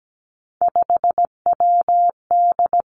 Morse code, heard as a keyed tone, about 17 wpm, 700 Hz. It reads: 5WD
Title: Three Challenging Letter-Number Combinations 17wpm